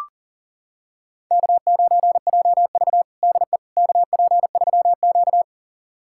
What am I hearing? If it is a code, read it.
Morse code, heard as a keyed tone, about 40 wpm, 700 Hz. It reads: K9JV DE KP3Q
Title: USA Callsign DE Callsign - 4 Characters - 40wpm